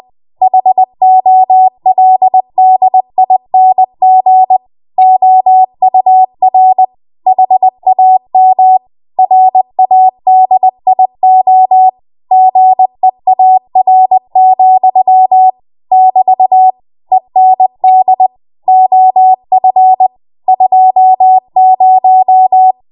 Morse code, heard as a keyed tone, about 20 wpm, 750 Hz. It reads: HOLDING OUR HAM RADIO GEAR, <BT> END OF 20
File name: Morse / 20WPM